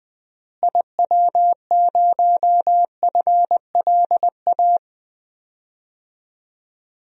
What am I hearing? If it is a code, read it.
Morse code, heard as a keyed tone, about 21 wpm, 700 Hz. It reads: IW0FLA
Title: DXCC-100 Call Signs - 6 Characters - Repeated 3x - 20wpm